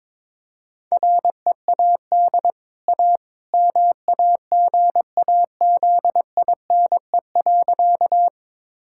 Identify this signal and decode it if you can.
Morse code, heard as a keyed tone, about 22 wpm, 690 Hz. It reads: READ A MAGAZINE.
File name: practice / 22wpm / Easy Sentences with Hints - 22wpm